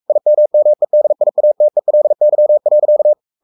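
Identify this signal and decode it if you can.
Morse code, heard as a keyed tone, about 44 wpm, 600 Hz. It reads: IMMEDIATELY.
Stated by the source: Morse code at 43 wpm sending Ken Follett's novel Never in English, from CW med Gnister.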